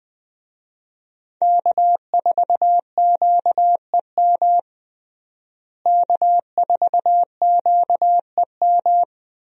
Morse code, heard as a keyed tone, about 21 wpm, 700 Hz. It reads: K4QEM K4QEM
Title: USA Call Signs - 5 Characters - Repeated 2x - 20wpm